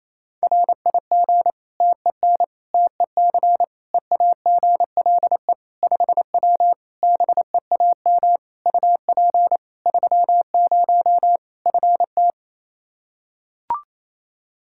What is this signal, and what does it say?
Morse code, heard as a keyed tone, about 28 wpm, 700 Hz. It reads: RIG TEN TEC EAGLE 5W BEAM UP 30 FT
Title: Practice QSO - Exchange 2 - Info - 28wpm